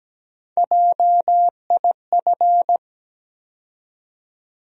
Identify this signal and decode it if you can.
Morse code, heard as a keyed tone, about 17 wpm, 700 Hz. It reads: JIF
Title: USA Call Sign Postfixes - 3 Letters 17wpm